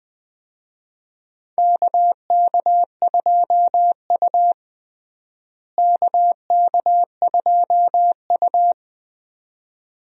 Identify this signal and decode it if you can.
Morse code, heard as a keyed tone, about 21 wpm, 700 Hz. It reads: KK2U KK2U
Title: USA Call Signs - 4 Characters - Repeated 2x - 20wpm